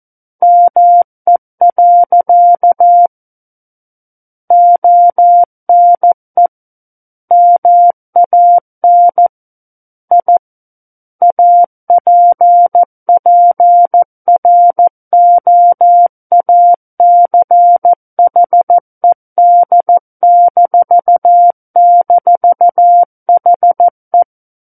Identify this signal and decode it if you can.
Morse code, heard as a keyed tone, about 14 wpm, 700 Hz. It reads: ME. ONE MAN I APPROACHED--HE
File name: Book One / 05-Chapter 5 - 14 WPM